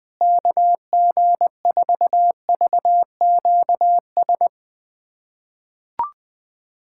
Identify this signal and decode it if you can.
Morse code, heard as a keyed tone, about 20 wpm, 700 Hz. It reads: KG4VQS
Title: USA Callsign DE Callsign 20wpm